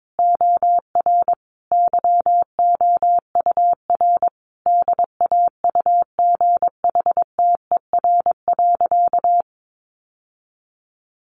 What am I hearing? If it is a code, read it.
Morse code, heard as a keyed tone, about 22 wpm, 700 Hz. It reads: OR YOUR DAUGHTER.
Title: Sentences from Top 1500 Words 22wpm